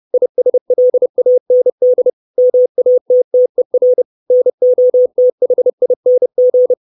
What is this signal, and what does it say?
Morse code, heard as a keyed tone, about 30 wpm, 500 Hz. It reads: ISLAND MATTER NOTHING
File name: 632 words / 632 Words 30WPM